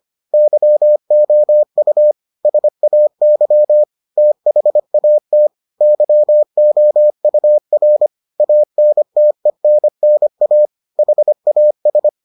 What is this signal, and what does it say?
Morse code, heard as a keyed tone, about 25 wpm, 600 Hz. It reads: YOU SAY THAT YOUR ANTENNA HAS